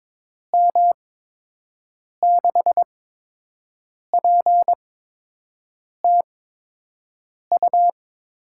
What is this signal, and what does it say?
Morse code, heard as a keyed tone, about 22 wpm, 700 Hz. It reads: M 6 P T U
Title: USA Call Signs - 6 Characters - 8x Character Spacing - 22wpm